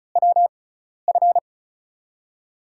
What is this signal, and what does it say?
Morse code, heard as a keyed tone, about 36 wpm, spaced as for 12 wpm, 700 Hz. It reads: WF
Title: USA Call Signs - 6 Characters - 6x Character Spacing - 35wpm